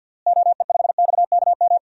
Morse code, heard as a keyed tone, about 53 wpm, 700 Hz. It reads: OE5XKM
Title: DXCC-100 Call Signs - 6 Characters - Repeated 3x - 50wpm